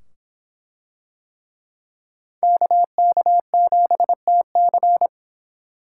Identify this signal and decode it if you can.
Morse code, heard as a keyed tone, about 24 wpm, 700 Hz. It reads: KK7TC